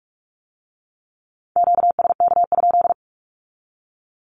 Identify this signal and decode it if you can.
Morse code, heard as a keyed tone, about 46 wpm, 700 Hz. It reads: QSK?
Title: Q-Codes with Spoken Meaning - 45wpm